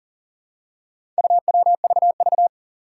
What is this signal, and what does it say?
Morse code, heard as a keyed tone, about 43 wpm, 700 Hz. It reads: UWVV
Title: Four Challenging Letter-Number Combinations 40wpm